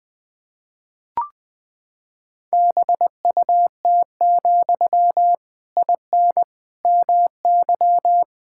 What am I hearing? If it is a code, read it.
Morse code, heard as a keyed tone, about 21 wpm, 700 Hz. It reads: BUT, IN MY
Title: Sentences from Top 1000 Words - Review Part 5 - 20wpm